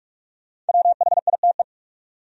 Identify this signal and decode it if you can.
Morse code, heard as a keyed tone, about 54 wpm, 700 Hz. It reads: WHITE
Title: Speed-Racing Top 300 Words 45wpm